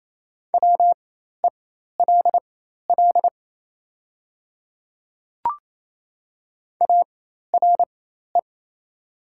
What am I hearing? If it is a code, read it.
Morse code, heard as a keyed tone, about 27 wpm, spaced as for 13 wpm, 700 Hz. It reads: WELL ARE
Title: 4x Character Spacing - Top 300 Words - 28wpm